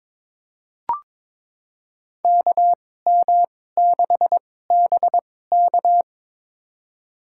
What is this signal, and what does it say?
Morse code, heard as a keyed tone, about 22 wpm, spaced as for 16 wpm, 700 Hz. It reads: KM6BK
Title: USA Call Signs - 5 Characters - 2x Character Spacing - 22wpm